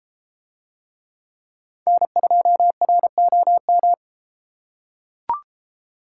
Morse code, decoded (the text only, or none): N2ROM